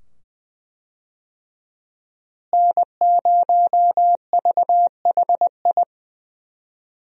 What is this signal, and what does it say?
Morse code, heard as a keyed tone, about 20 wpm, 700 Hz. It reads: N0VHI
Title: USA Call Signs - 5 Characters - Repeated 2x - 20wpm